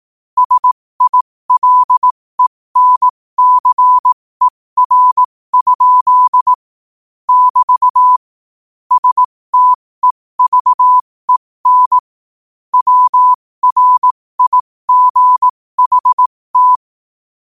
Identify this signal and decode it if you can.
Morse code, heard as a keyed tone, about 18 wpm, spaced as for 15 wpm, 1000 Hz. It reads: SILENCER? <BT> STEVEN WRIGHT